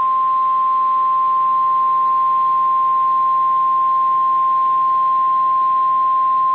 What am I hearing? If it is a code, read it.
Emergency Alert System (SAME) tones (the 1050 Hz attention tone). It is heard from 0.5 to 5.4 s.